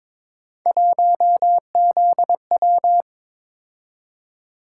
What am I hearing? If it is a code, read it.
Morse code, heard as a keyed tone, about 22 wpm, 700 Hz. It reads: 1ZW
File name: practice / 22wpm / Three Letter-Number Combinations - 22wpm